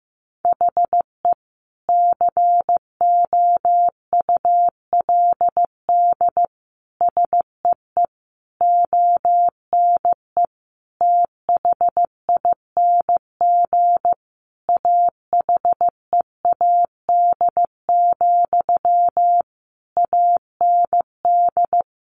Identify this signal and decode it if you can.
Morse code, heard as a keyed tone, about 15 wpm, 700 Hz. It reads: HE COULD SEE ONE THING AHEAD, AND
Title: Sentences from Top 1100 Words 15wpm